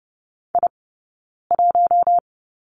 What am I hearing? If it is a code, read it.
Morse code, heard as a keyed tone, about 30 wpm, 700 Hz. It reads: I 1
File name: Lesson 20d - Number 3 - Two Characters 30wpm